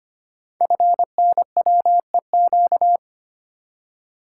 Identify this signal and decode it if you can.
Morse code, heard as a keyed tone, about 25 wpm, 700 Hz. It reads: FNWEQ
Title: Code Groups - 5 Letters - 25wpm